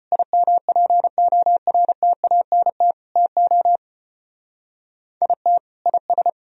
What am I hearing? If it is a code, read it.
Morse code, heard as a keyed tone, about 32 wpm, 700 Hz. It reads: IMPORTANT TO IT IS